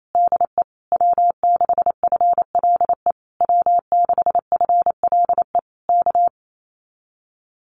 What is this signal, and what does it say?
Morse code, heard as a keyed tone, about 28 wpm, 700 Hz. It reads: DE W6FLE W6FLE K
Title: Common QSO Phrases Encore 28wpm